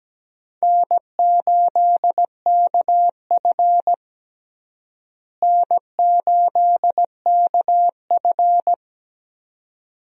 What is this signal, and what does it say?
Morse code, heard as a keyed tone, about 17 wpm, 700 Hz. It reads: N8KF N8KF
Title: USA Call Signs - 4 Characters - Repeated 2x - 17wpm